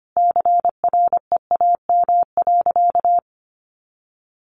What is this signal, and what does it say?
Morse code, heard as a keyed tone, about 25 wpm, 700 Hz. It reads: CREAM.
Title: Sentences from Top 1500 Words 25wpm